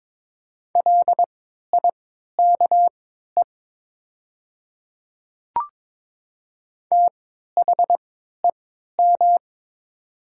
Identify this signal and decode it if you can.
Morse code, heard as a keyed tone, about 22 wpm, spaced as for 12 wpm, 700 Hz. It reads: LIKE THEM
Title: 3x Character Spacing - Top 100 Words - 22wpm